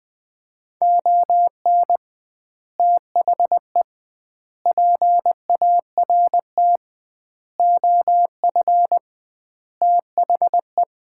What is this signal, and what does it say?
Morse code, heard as a keyed tone, about 21 wpm, 700 Hz. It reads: ON THE PART OF THE
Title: 2x Word-Spacing - Sets of 5 Words - 20wpm